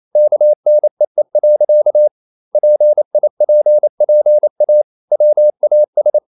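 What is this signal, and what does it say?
Morse code, heard as a keyed tone, about 28 wpm, 600 Hz. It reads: KNEE. PIPPA WAS
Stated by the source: Morse code practice at 28 wpm, text of Ken Follett's novel Never in English, from CW med Gnister.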